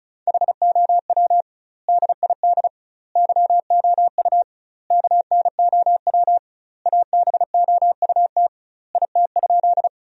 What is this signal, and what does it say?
Morse code, heard as a keyed tone, about 34 wpm, 700 Hz. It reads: HOW DID YOU KNOW ABOUT IT?
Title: 2x Word-Spacing - Sentences from Top 100 Words - 35wpm